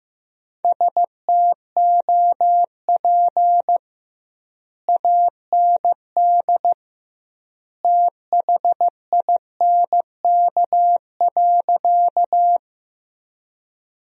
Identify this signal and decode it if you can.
Morse code, heard as a keyed tone, about 15 wpm, 700 Hz. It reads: STOP AND THINK.
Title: 2x Word-Spacing - Sentences from Top 300 Words - 15wpm